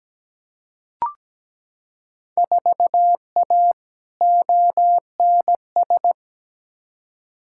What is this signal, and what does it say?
Morse code, heard as a keyed tone, about 17 wpm, 700 Hz. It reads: 4A ONS
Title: Field Day Exchanges - 17wpm